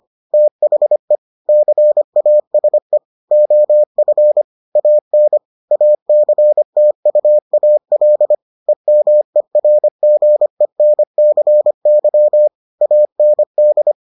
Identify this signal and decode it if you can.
Morse code, heard as a keyed tone, about 25 wpm, 600 Hz. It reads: THE CASE OF AN ACTUAL EMERGENCY AND